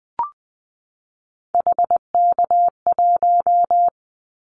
Morse code, heard as a keyed tone, about 20 wpm, 700 Hz. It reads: HK1
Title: Callsign Prefixes for Top 100 DXCC - 20wpm